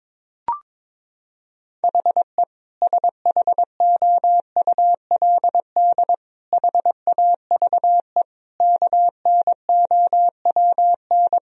Morse code, heard as a keyed tone, about 22 wpm, 700 Hz. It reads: HE SHOULD HAVE KNOWN
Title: Sentences from Top 1000 Words - Review Part 3 - 22wpm